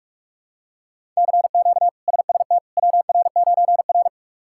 Morse code, heard as a keyed tone, about 41 wpm, 700 Hz. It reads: CQ SST WR9R
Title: K1USN SST QSO Practice - 45wpm